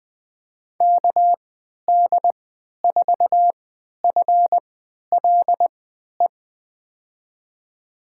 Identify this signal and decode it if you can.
Morse code, heard as a keyed tone, about 21 wpm, spaced as for 11 wpm, 700 Hz. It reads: KD4FLE
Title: USA Call Signs - 6 Characters - 3x Character Spacing - 20wpm